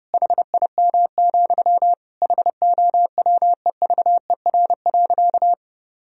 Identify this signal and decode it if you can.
Morse code, heard as a keyed tone, about 30 wpm, 700 Hz. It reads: HIM, HOWEVER.
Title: Sentences from Top 1000 Words - Review Part 2 - 30wpm